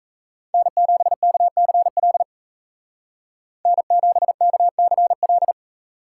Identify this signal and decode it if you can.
Morse code, heard as a keyed tone, about 33 wpm, 700 Hz. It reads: N7KCL N7KCL
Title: Speed Racing - USA 1x3 Call Signs 28wpm